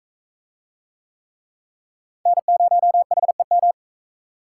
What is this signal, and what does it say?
Morse code, heard as a keyed tone, about 40 wpm, 700 Hz. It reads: N0HEM